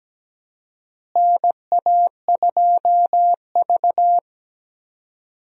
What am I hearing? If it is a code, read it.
Morse code, heard as a keyed tone, about 17 wpm, 700 Hz. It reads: NA2V
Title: USA Call Signs - 4 Characters - Repeated 3x - 17wpm